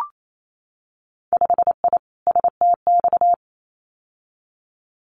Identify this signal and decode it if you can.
Morse code, heard as a keyed tone, about 28 wpm, 700 Hz. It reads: 5I STX
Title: Winter Field Day Exchanges Encore 28wpm